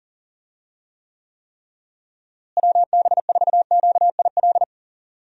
Morse code, heard as a keyed tone, about 40 wpm, 700 Hz. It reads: WB4QIL
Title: USA Call Signs - 6 Characters - Repeated 3x - 40wpm